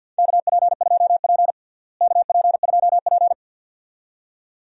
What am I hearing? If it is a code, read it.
Morse code, heard as a keyed tone, about 53 wpm, 700 Hz. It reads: KP2P KP2P
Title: DXCC-100 Call Signs - 4 Characters - Repeated 2x - 50wpm